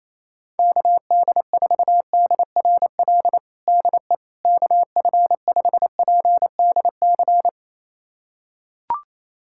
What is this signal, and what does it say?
Morse code, heard as a keyed tone, about 27 wpm, 700 Hz. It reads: KD4DRL DE KF5PDC
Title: USA Callsign DE Callsign - 6 Characters - 28wpm